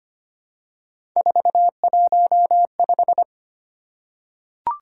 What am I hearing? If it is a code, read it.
Morse code, heard as a keyed tone, about 25 wpm, 700 Hz. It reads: 415
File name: Three Letter-Number Combinations - 25wpm